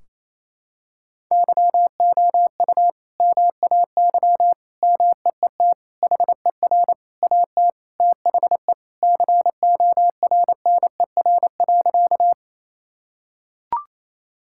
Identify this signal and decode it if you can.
Morse code, heard as a keyed tone, about 28 wpm, 700 Hz. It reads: YOU MAY MEET HER AT THE CORNER.
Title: Sentences from Top 1600 Words 28wpm